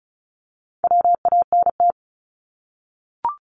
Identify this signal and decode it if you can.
Morse code, heard as a keyed tone, about 36 wpm, 700 Hz. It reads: WANT
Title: Multi-letter Phonemes 35wpm